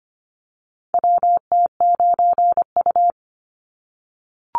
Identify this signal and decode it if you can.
Morse code, heard as a keyed tone, about 25 wpm, 700 Hz. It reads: WT9U
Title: Call Signs - 25wpm